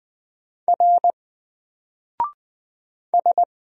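Morse code, heard as a keyed tone, about 20 wpm, 700 Hz. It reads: R S
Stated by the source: Single letter and number Morse drill at 20 wpm, from N4LCD.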